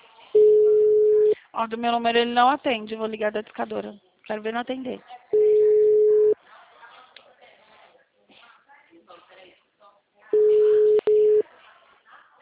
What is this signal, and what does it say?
Ringback tone (Europe, 425 Hz), with speech.